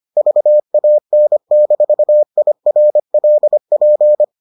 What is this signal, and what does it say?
Morse code, heard as a keyed tone, about 25 wpm, 600 Hz. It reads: VAN-IRLP